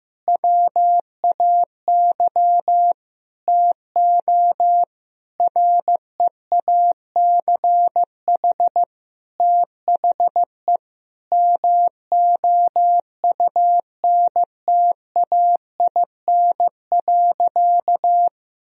Morse code, heard as a keyed tone, about 15 wpm, 700 Hz. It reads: WAY TO REACH THE MOUNTAIN.
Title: Sentences from Top 1000 Words - Review Part 2 - 15wpm